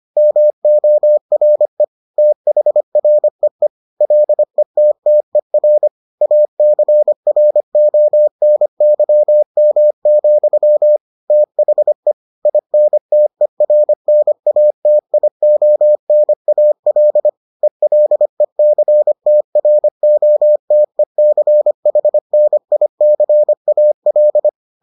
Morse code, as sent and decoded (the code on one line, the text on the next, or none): -- --- .-. / - .... .-. . / .-.. - - . .-. / .- -.-. .-. --- -. -.-- -- --..-- / - .... / .. -. - . .-. -. .- - .. --- -. .- .-.. / . .-.. -.-. - .-. --- - . -.-. .... -. .. -.-. .- .-..
MORE THREE LETTER ACRONYM, THE INTERNATIONAL ELECTROTECHNICAL